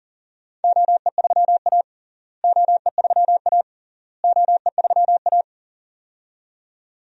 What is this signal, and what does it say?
Morse code, heard as a keyed tone, about 40 wpm, 700 Hz. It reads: OE3A OE3A OE3A
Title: DXCC-100 Call Signs - 4 Characters - Repeated 3x - 40wpm